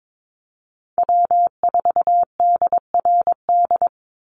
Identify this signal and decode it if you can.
Morse code, heard as a keyed tone, about 22 wpm, 700 Hz. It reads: W4DRD
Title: USA Call Signs 22wpm